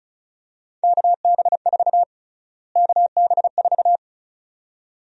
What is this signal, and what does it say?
Morse code, heard as a keyed tone, about 34 wpm, 700 Hz. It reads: KB4 KB4
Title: USA Call Sign Prefixes - Repeated 2x - 35wpm